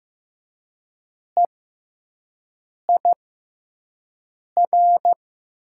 Morse code, heard as a keyed tone, about 15 wpm, spaced as for 5 wpm, 700 Hz. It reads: EIR
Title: 6x Character Spacing - Top 300 Words - 15wpm